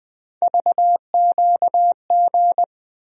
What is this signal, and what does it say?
Morse code, heard as a keyed tone, about 20 wpm, 700 Hz. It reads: VQG
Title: USA Call Sign Postfixes - 3 Letters 20wpm